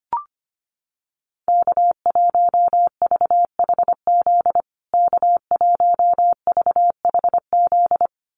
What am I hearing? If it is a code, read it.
Morse code, heard as a keyed tone, about 25 wpm, 700 Hz. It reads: K1VHZ K1VHZ
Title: USA Call Signs Repeated 2x 25wpm